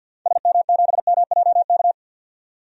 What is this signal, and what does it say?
Morse code, heard as a keyed tone, about 50 wpm, 700 Hz. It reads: SM7MJK
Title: DXCC-100 Call Signs - 6 Characters - 50wpm